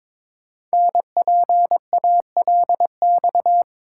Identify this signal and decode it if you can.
Morse code, heard as a keyed tone, about 22 wpm, 700 Hz. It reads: NPALX
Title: Code Groups - 5 Letter:Numbers - 22wpm